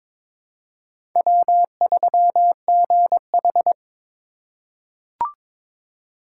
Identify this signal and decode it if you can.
Morse code, heard as a keyed tone, about 21 wpm, 700 Hz. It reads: W3GH
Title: USA Call Signs - 4 Characters - 2x Character Spacing - 22wpm